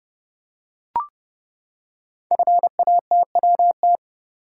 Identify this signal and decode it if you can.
Morse code, heard as a keyed tone, about 30 wpm, 700 Hz. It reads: FATWT
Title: Code Groups - 5 Letters - 30wpm